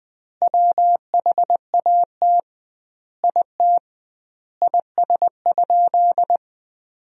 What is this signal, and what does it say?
Morse code, heard as a keyed tone, about 20 wpm, 700 Hz. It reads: WHAT IT IS?
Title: 2x Word-Spacing - Sentences from Top 100 Words - 20wpm